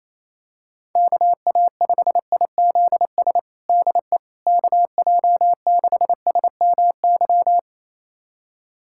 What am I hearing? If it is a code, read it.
Morse code, heard as a keyed tone, about 28 wpm, 700 Hz. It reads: KA5IZS DE KJ6SMY
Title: USA Callsign DE Callsign - 6 Characters - 28wpm